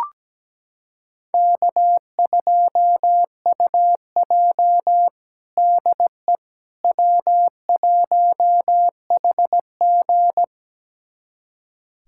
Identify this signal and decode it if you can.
Morse code, heard as a keyed tone, about 17 wpm, 700 Hz. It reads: K2UJ DE W1HG
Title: USA Callsign DE Callsign - 4 Characters - 17wpm